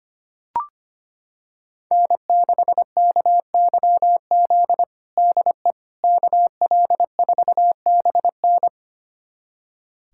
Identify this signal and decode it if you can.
Morse code, heard as a keyed tone, about 25 wpm, 700 Hz. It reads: N6KYZ DE KL4BN
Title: USA Callsign DE Callsign - 5 Characters - 25wpm